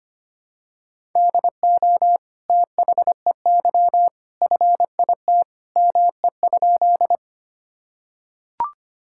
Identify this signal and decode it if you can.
Morse code, heard as a keyed tone, about 25 wpm, 700 Hz. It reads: DO THEY FIT ME?
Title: Sentences from Top 1000 Words - Review Part 5 - 25wpm